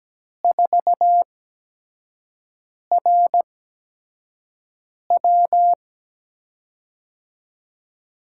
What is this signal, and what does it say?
Morse code, heard as a keyed tone, about 17 wpm, spaced as for 5 wpm, 700 Hz. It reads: 4RW
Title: USA Call Signs - 5 Characters - 8x Character Spacing - 17wpm